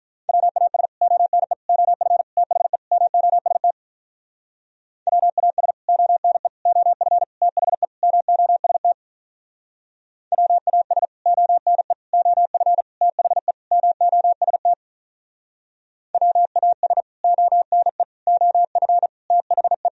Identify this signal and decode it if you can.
Morse code, heard as a keyed tone, about 47 wpm, 700 Hz. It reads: WAS ONE OF THE MOST WAS ONE OF THE MOST WAS ONE OF THE MOST WAS ONE OF THE